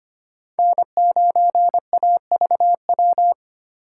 Morse code, heard as a keyed tone, about 25 wpm, 700 Hz. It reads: N9AVW
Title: USA Call Signs - 5 Characters - 4x Character Spacing - 25wpm